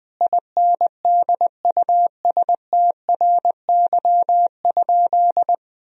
Morse code, heard as a keyed tone, about 21 wpm, 700 Hz. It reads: INDUSTRY?
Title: Sentences from Top 1000 Words - Review Part 4 - 20wpm